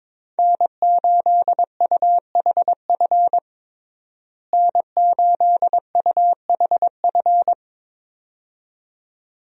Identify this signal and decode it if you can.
Morse code, heard as a keyed tone, about 22 wpm, 700 Hz. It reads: N8UHF N8UHF
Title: USA Call Signs - 5 Characters - Repeated 2x - 22wpm